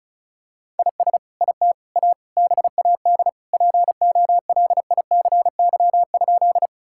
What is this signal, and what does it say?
Morse code, heard as a keyed tone, about 36 wpm, 700 Hz. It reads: IS IT A BAD POLICY?
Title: Sentences from Top 1000 Words - Review Part 5 - 35wpm